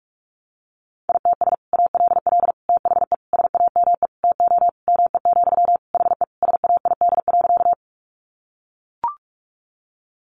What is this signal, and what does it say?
Morse code, heard as a keyed tone, about 44 wpm, 700 Hz. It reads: ITS ALL THE SAME TO ME, HE SAID.
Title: Sentences from Top 600 Words 45wpm